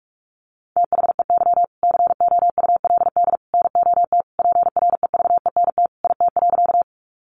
Morse code, heard as a keyed tone, about 46 wpm, 700 Hz. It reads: THEY COULD NOT PREVENT IT.
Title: Sentences from Top 1100 Words 45wpm